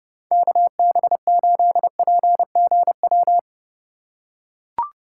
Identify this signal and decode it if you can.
Morse code, heard as a keyed tone, about 30 wpm, 700 Hz. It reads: KB8PGW
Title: K1USN SST QSO Practice - 30wpm